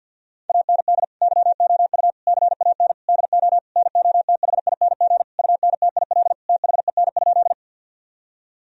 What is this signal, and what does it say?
Morse code, heard as a keyed tone, about 52 wpm, 700 Hz. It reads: AND YOU CAN DO NOTHING UNTIL THEN?
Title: Sentences from Top 1000 Words - Review Part 1 - 50wpm